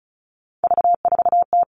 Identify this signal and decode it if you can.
Morse code, heard as a keyed tone, about 36 wpm, 700 Hz. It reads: V4T
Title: Three Challenging Letter-Number Combinations 35wpm